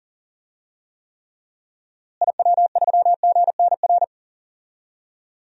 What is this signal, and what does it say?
Morse code, heard as a keyed tone, about 40 wpm, 700 Hz. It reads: IW3GNR
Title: DXCC-100 Call Signs - 6 Characters - Repeated 2x - 40wpm